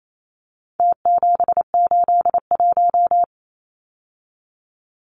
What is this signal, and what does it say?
Morse code, heard as a keyed tone, about 28 wpm, 700 Hz. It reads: T781
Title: Numbers - Contest Style - 4-Digit 28wpm